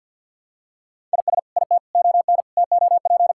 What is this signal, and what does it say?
Morse code, heard as a keyed tone, about 52 wpm, 700 Hz. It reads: IS IT ON TOP
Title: Sentences from Top 1000 Words - Review Part 6 - 50wpm